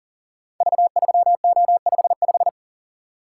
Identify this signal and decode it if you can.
Morse code, heard as a keyed tone, about 43 wpm, 700 Hz. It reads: V3O55
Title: Code Groups - 5 Letter:Numbers - 40wpm